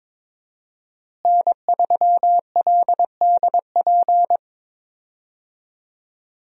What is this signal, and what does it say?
Morse code, heard as a keyed tone, about 22 wpm, 700 Hz. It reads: N3LDP
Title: USA Call Signs - 5 Characters - Repeated 3x - 22wpm